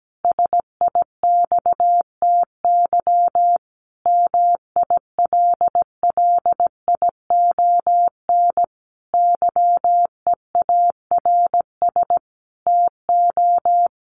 Morse code, heard as a keyed tone, about 17 wpm, 700 Hz. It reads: SIXTY MILLION YEARS TO
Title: A Sound Of Thunder 17wpm